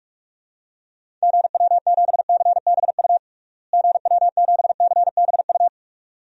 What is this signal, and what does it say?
Morse code, heard as a keyed tone, about 47 wpm, 700 Hz. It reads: GW7CBU GW7CBU
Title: DXCC-100 Call Signs - 6 Characters - Repeated 2x - 45wpm